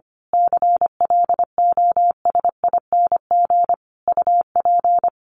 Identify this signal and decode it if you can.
Morse code, heard as a keyed tone, about 25 wpm, 700 Hz. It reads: CLOSING UP
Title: A Sound Of Thunder 25wpm